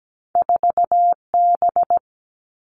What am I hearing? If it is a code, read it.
Morse code, heard as a keyed tone, about 17 wpm, 700 Hz. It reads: 4B
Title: All Cut Numbers - 2-Digit 17wpm